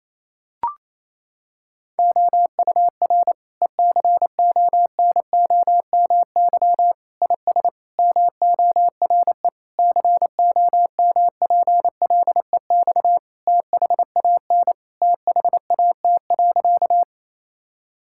Morse code, heard as a keyed tone, about 27 wpm, 700 Hz. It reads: OUR ECONOMY IS MORE COMPLEX THAN THAT.
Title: Sentences from Top 1000 Words - Review Part 6 - 28wpm